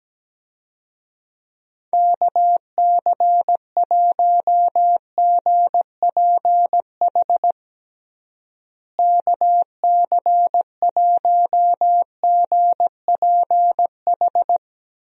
Morse code, heard as a keyed tone, about 17 wpm, 700 Hz. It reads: KC1GPH KC1GPH
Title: USA Call Signs - 6 Characters - Repeated 3x - 17wpm